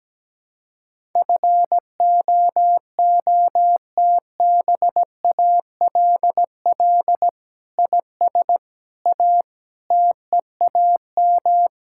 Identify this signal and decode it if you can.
Morse code, heard as a keyed tone, about 17 wpm, 700 Hz. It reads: FOOTBALL IS A TEAM
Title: Sentences from Top 1600 Words 17wpm